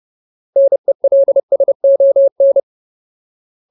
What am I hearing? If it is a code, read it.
Morse code, heard as a keyed tone, about 30 wpm, 550 Hz. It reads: NELSON
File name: Ultimate Word List / 30WPM Ultimate